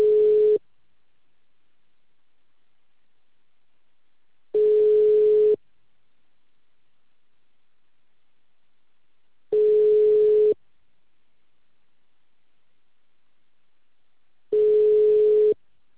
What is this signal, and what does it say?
Ringback tone (Europe, 425 Hz).